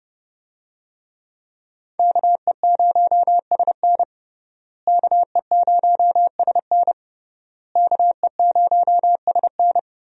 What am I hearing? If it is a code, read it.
Morse code, heard as a keyed tone, about 30 wpm, 700 Hz. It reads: KE0SN KE0SN KE0SN
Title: USA Call Signs - 5 Characters - Repeated 3x - 30wpm